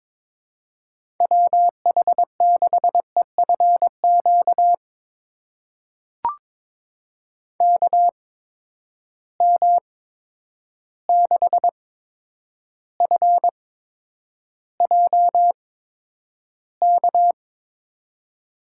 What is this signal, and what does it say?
Morse code, heard as a keyed tone, about 22 wpm, spaced as for 9 wpm, 700 Hz. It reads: WH6EFQ KM6FJK
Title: USA Call Signs - 6 Characters - 8x Character Spacing - 22wpm